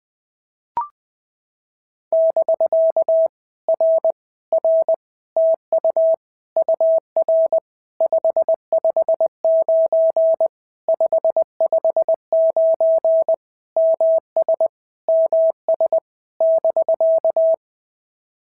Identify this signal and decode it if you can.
Morse code, heard as a keyed tone, about 20 wpm, 650 Hz. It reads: <BK> R R TU UR 559 559 MS MS <BK>
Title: POTA QSO Practice - 20wpm